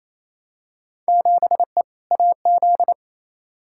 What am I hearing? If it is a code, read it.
Morse code, heard as a keyed tone, about 28 wpm, 700 Hz. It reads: 7E AZ
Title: Field Day Exchanges Encore - 28wpm